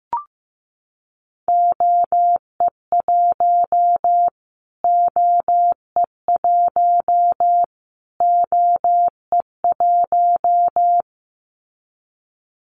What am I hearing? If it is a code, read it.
Morse code, heard as a keyed tone, about 15 wpm, 700 Hz. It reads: OE1 OE1 OE1
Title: Callsign Prefixes for Top 25 DXCC - Repeated 3x - 15wpm